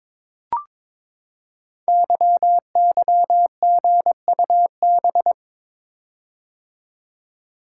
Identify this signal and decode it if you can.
Morse code, heard as a keyed tone, about 22 wpm, 700 Hz. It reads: YYGUB
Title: Code Groups - 5 Letters - 22wpm